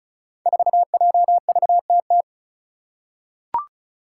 Morse code, heard as a keyed tone, about 35 wpm, 700 Hz. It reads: 4JVTT
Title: Five Challenging Letter-Number Combinations 35wpm